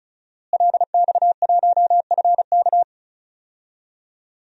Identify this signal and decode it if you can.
Morse code, heard as a keyed tone, about 34 wpm, 700 Hz. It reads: LX1FK
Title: DXCC-100 Call Signs - 5 Characters - Repeated 2x - 35wpm